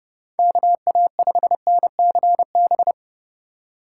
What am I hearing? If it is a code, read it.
Morse code, heard as a keyed tone, about 30 wpm, 700 Hz. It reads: KA5NCB
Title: USA Call Signs - 2x Character Spacing - 30wpm